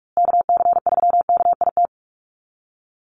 Morse code, heard as a keyed tone, about 44 wpm, 700 Hz. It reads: KC3KIT
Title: USA Call Signs Repeated 3x 45wpm